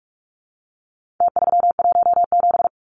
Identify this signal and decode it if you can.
Morse code, heard as a keyed tone, about 44 wpm, 700 Hz. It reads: T317
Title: Numbers - Contest Style - 4-Digit 45wpm